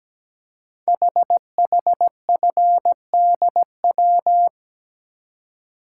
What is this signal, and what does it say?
Morse code, heard as a keyed tone, about 17 wpm, 700 Hz. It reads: HHFDW
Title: Code Groups - 5 Letters - 17wpm